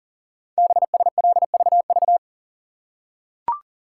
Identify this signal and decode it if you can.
Morse code, heard as a keyed tone, about 40 wpm, 700 Hz. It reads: BSLVV
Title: Code Groups - 5 Letters - 40wpm